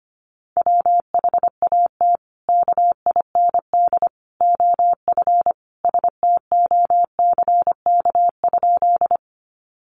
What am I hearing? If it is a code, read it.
Morse code, heard as a keyed tone, about 25 wpm, 700 Hz. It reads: WHAT KIND OF STOCK?
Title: Sentences from Top 800 Words 25wpm